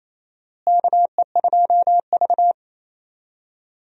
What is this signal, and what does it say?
Morse code, heard as a keyed tone, about 28 wpm, 700 Hz. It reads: KE2V
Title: USA Call Signs - 4 Characters - Repeated 2x - 28wpm